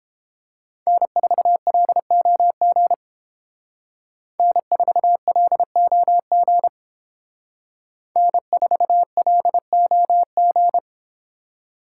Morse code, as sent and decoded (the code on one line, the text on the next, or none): -. ....- .-.. --- --. / -. ....- .-.. --- --. / -. ....- .-.. --- --.
N4LOG N4LOG N4LOG